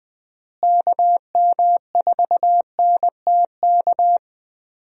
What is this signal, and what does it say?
Morse code, heard as a keyed tone, about 21 wpm, 700 Hz. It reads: KM4NTK
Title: USA Call Signs - 6 Characters - Repeated 2x - 20wpm